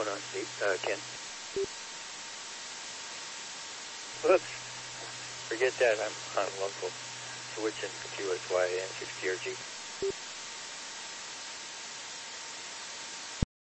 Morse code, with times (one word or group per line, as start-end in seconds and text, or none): none